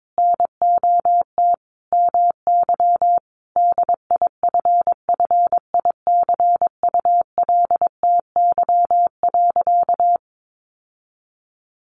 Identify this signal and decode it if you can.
Morse code, heard as a keyed tone, about 22 wpm, 700 Hz. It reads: NOT MY DIFFICULTY.
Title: Sentences from Top 1500 Words 22wpm